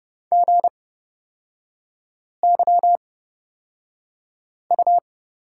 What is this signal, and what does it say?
Morse code, heard as a keyed tone, about 30 wpm, spaced as for 5 wpm, 700 Hz. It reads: GYU
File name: course / 30:10wpm / 236 Lesson 38e - Letter q - Three Characters 30wpm